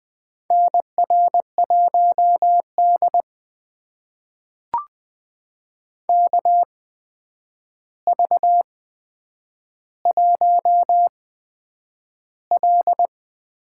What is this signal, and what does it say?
Morse code, heard as a keyed tone, about 20 wpm, spaced as for 9 wpm, 700 Hz. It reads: NR1D KV1L
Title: USA Call Signs - 4 Characters - 8x Character Spacing - 20wpm